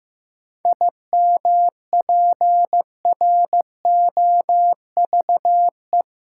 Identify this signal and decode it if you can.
Morse code, heard as a keyed tone, about 15 wpm, 700 Hz. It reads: IMPROVE
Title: Sentences from Top 800 Words - New Word Drill - 15wpm